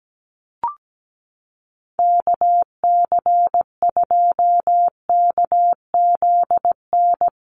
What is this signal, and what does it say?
Morse code, heard as a keyed tone, about 17 wpm, 700 Hz. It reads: KC2KZN
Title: USA Call Signs Repeated 3x 17wpm